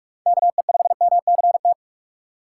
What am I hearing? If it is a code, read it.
Morse code, heard as a keyed tone, about 46 wpm, 700 Hz. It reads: KE5MCT
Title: USA Call Signs - 6 Characters - 8x Character Spacing - 45wpm